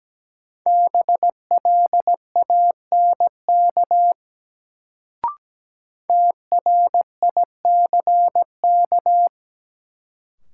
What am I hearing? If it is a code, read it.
Morse code, heard as a keyed tone, about 17 wpm, 700 Hz. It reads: BLANK TRICK
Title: Sentences from Top 1500 Words - New Word Drill - 17wpm